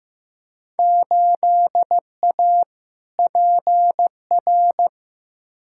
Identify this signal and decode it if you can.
Morse code, heard as a keyed tone, about 15 wpm, 700 Hz. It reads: 8A PR
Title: Field Day Exchanges Encore - 15wpm